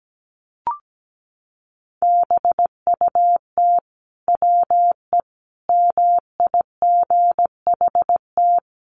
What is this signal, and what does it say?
Morse code, heard as a keyed tone, about 17 wpm, 700 Hz. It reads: BUT WE MIGHT
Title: Sentences from Top 1300 Words 17wpm